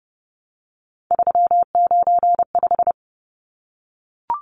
Morse code, heard as a keyed tone, about 30 wpm, 700 Hz. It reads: 395
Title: Numbers - 3-Digit 30wpm